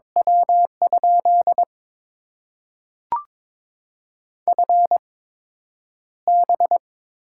Morse code, heard as a keyed tone, about 21 wpm, spaced as for 9 wpm, 700 Hz. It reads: W? FB
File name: Top 20 QSO Elements - 8x Character Spacing - 22wpm